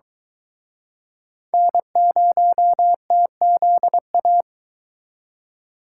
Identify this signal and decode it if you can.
Morse code, heard as a keyed tone, about 23 wpm, 700 Hz. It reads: N0TZA